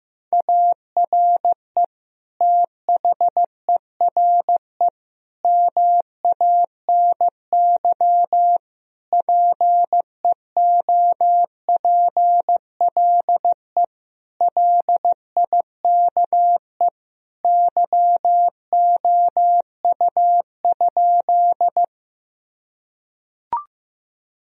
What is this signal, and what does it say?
Morse code, heard as a keyed tone, about 15 wpm, 700 Hz. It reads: ARE THERE MANY PEOPLE LIKE YOU?
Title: Sentences from Top 1000 Words - Review Part 1 - 15wpm